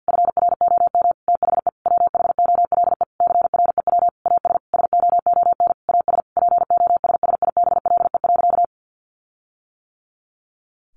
Morse code, heard as a keyed tone, about 50 wpm, 700 Hz. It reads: FROM THE WHOLE CREW AS SOON AS POSSIBLE.
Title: Sentences from Top 1500 Words 50wpm